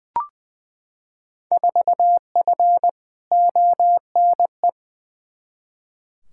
Morse code, heard as a keyed tone, about 20 wpm, 700 Hz. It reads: 4F ONE
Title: Field Day Exchanges Encore - 20wpm